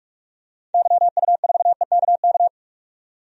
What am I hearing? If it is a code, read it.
Morse code, heard as a keyed tone, about 47 wpm, 700 Hz. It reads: YU4EKK
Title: DXCC-100 Call Signs - 6 Characters - 45wpm